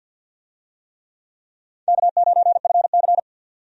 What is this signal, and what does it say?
Morse code, heard as a keyed tone, about 52 wpm, 700 Hz. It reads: K9FC
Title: USA Call Signs - 4 Characters - 50wpm